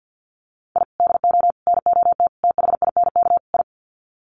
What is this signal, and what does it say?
Morse code, heard as a keyed tone, about 50 wpm, 700 Hz. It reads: I DO NOT THINK I
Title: Sets of 5 Words 50wpm